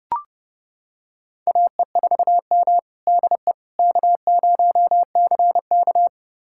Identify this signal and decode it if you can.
Morse code, heard as a keyed tone, about 31 wpm, 700 Hz. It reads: AE4M DE K0CK
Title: USA Callsign DE Callsign - 4 Characters - 30wpm